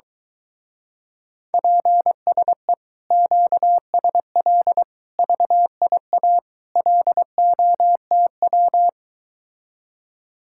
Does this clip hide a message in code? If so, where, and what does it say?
Yes: Morse code from 1.5 to 8.9 s, heard as a keyed tone, about 22 wpm, 700 Hz. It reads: PSE QSL VIA LOTW